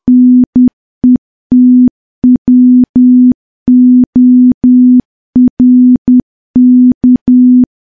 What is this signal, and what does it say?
Morse code, heard as a keyed tone, about 10 wpm, 250 Hz. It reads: NETWORK
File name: morse / morse-training-10WPM